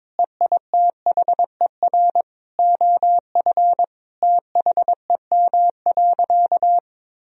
Morse code, heard as a keyed tone, about 22 wpm, 700 Hz. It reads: EITHER OF THEM.
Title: Sentences from Top 1000 Words - Review Part 4 - 22wpm